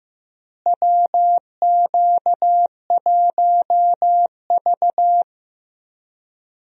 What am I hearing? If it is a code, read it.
Morse code, heard as a keyed tone, about 15 wpm, 700 Hz. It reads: WQ1V
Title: USA Call Signs - 4 Characters - Repeated 3x - 15wpm